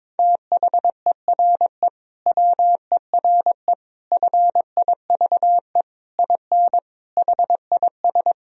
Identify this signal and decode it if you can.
Morse code, heard as a keyed tone, about 22 wpm, 700 Hz. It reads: THERE WERE FIVE IN HIS
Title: Sentences from Top 1000 Words - Review Part 2 - 22wpm